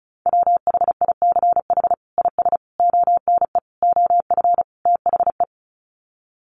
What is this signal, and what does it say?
Morse code, heard as a keyed tone, about 34 wpm, 700 Hz. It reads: WHICH IS ONE OF THE
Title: Sets of 5 Words Encore 35wpm